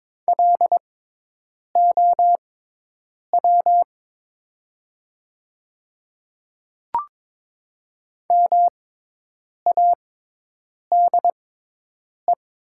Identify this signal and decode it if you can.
Morse code, heard as a keyed tone, about 22 wpm, spaced as for 8 wpm, 700 Hz. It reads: LOW MADE
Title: 6x Character Spacing - Top 200 Words - 22wpm